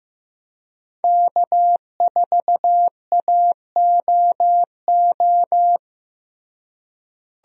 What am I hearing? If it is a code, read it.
Morse code, heard as a keyed tone, about 15 wpm, 700 Hz. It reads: K4AOO
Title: USA Call Signs - 5 Characters - Repeated 3x - 15wpm